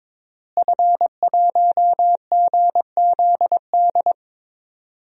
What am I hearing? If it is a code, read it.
Morse code, heard as a keyed tone, about 22 wpm, 700 Hz. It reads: F1GZD
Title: DXCC-100 Call Signs - 5 Characters - Repeated 2x - 22wpm